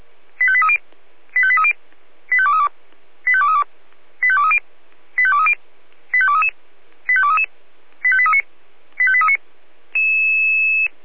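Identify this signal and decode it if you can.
Five-tone selective call (ZVEI), heard as quick runs of five tones, tones of about 70 ms. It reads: 85830 85830 85232 85232 85230 85230 85233 85233 85839 85839
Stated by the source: Five-tone ZVEI selective call received on an SDR.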